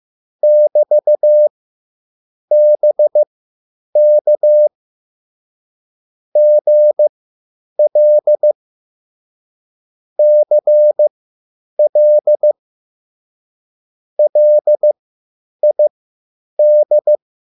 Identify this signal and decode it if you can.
Morse code, heard as a keyed tone, about 15 wpm, spaced as for 8 wpm, 600 Hz. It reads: <BT>BK GL CL LID